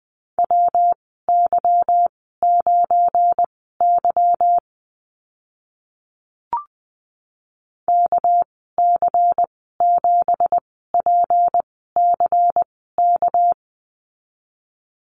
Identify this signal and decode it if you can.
Morse code, heard as a keyed tone, about 20 wpm, spaced as for 14 wpm, 700 Hz. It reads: WY9Y KC7PCK
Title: 231 Lesson 37g - Letter x - Call Signs 20wpm